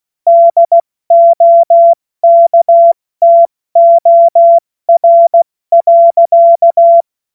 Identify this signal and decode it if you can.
Morse code, heard as a keyed tone, about 16 wpm, 680 Hz. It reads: DOKTOR.
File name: Morsekurs / Witze / Witz 19 März 2024 70 F80